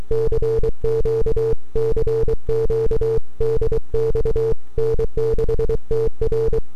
Morse code, heard as a keyed tone, about 23 wpm, 450 Hz. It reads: CQ CQ DX N6TR
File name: n6tr / 12Oct2009-160